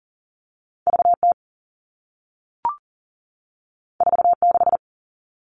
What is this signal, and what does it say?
Morse code, heard as a keyed tone, about 38 wpm, 700 Hz. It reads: VT 46
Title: Two Challenging Letter-Number Combinations 40wpm